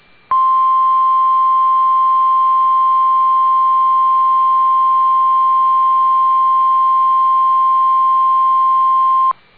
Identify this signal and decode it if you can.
Emergency Alert System (SAME) tones (the 1050 Hz attention tone). It is heard from 0.2 to 9.4 s.